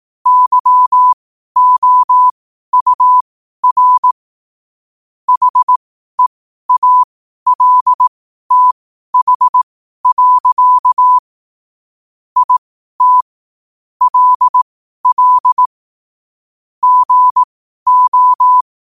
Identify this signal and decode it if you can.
Morse code, heard as a keyed tone, about 18 wpm, spaced as for 12 wpm, 1000 Hz. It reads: YOUR HEALTH. IT LL GO